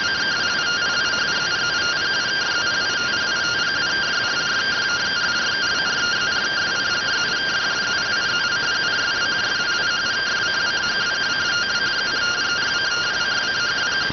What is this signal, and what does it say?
Two-tone teleprinter signal, about 170 Hz shift.